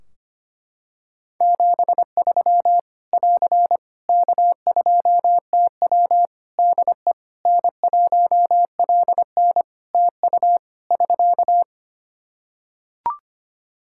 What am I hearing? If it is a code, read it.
Morse code, heard as a keyed tone, about 25 wpm, 700 Hz. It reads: 73 <AR> K2TW DE N1LN TU <SK>
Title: Practice QSO - Exchange 4 - Ending - 25wpm